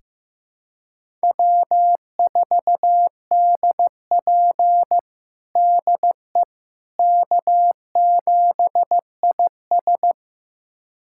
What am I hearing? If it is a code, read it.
Morse code, heard as a keyed tone, about 15 wpm, 700 Hz. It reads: W4DP DE K7IS
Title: USA Callsign DE Callsign - 4 Characters - 15wpm